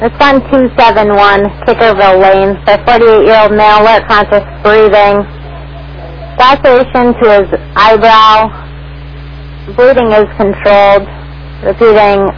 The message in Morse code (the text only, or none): none